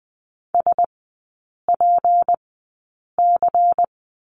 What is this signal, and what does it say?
Morse code, heard as a keyed tone, about 20 wpm, spaced as for 8 wpm, 700 Hz. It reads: SPC
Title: 212 Lesson 34e - Number 8 - Three Characters 20wpm